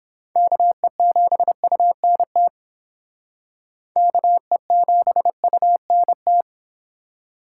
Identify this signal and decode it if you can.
Morse code, heard as a keyed tone, about 31 wpm, 700 Hz. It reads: KE7UNT KE7UNT